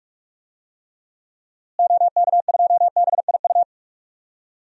Morse code, heard as a keyed tone, about 46 wpm, 700 Hz. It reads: OK2BIU
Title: DXCC-100 Call Signs - 6 Characters - Repeated 2x - 45wpm